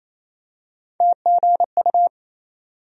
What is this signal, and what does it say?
Morse code, heard as a keyed tone, about 28 wpm, 700 Hz. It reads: TGU
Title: USA Call Sign Postfixes - 3 Letters 28wpm